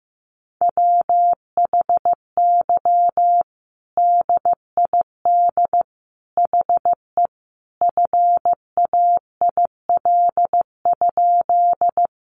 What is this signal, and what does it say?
Morse code, heard as a keyed tone, about 14 wpm, 700 Hz. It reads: WHY DID HE FAIL?
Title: Sentences from Top 1100 Words 15wpm